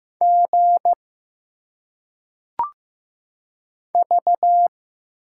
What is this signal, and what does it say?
Morse code, heard as a keyed tone, about 15 wpm, 700 Hz. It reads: G V
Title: Single Letters - Phonetic Pronunciation 15wpm